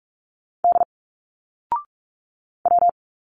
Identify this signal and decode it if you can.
Morse code, heard as a keyed tone, about 44 wpm, 700 Hz. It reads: D W
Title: Single Letters 45wpm